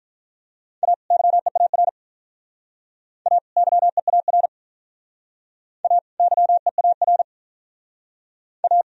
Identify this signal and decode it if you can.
Morse code, heard as a keyed tone, about 44 wpm, 700 Hz. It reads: A YEAR A YEAR A YEAR A